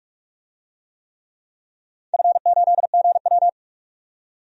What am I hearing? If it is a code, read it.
Morse code, heard as a keyed tone, about 46 wpm, 700 Hz. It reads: F8GW
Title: DXCC-100 Call Signs - 4 Characters - 45wpm